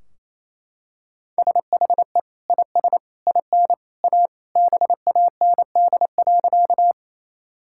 Morse code, heard as a keyed tone, about 27 wpm, 700 Hz. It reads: SHE IS IN A BAND.
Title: Sentences from Top 1600 Words 28wpm